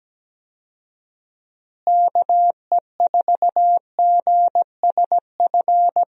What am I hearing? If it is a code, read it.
Morse code, heard as a keyed tone, about 17 wpm, 700 Hz. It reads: KE4GSF
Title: USA Call Signs - 6 Characters - Repeated 2x - 17wpm